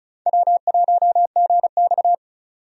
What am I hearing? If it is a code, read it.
Morse code, heard as a keyed tone, about 36 wpm, 700 Hz. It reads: W1GX
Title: USA Call Signs - 4 Characters - 35wpm